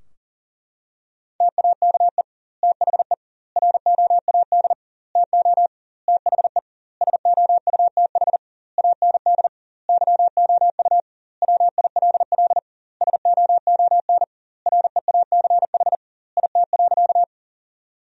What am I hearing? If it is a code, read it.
Morse code, heard as a keyed tone, about 40 wpm, 700 Hz. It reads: TAKE THE ROAD TO THE SOUTH AND YOU WILL SOON REACH IT.
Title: 2x Word-Spacing - Sentences from Top 400 Words - 40wpm